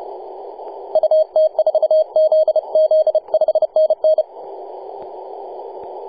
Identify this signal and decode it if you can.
Morse code, heard as a keyed tone, about 30 wpm, 630 Hz, with radio noise. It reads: UT4ZZ 5NN
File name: posts / 2015-nov-18-0552z-vk9wa-24-894mhz-cw